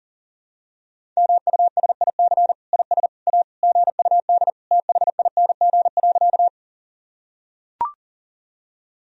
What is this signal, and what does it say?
Morse code, heard as a keyed tone, about 40 wpm, 700 Hz. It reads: MUSIC IS A GUD THING.
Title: Sentences from Top 300 Words and Abbreviations 40wpm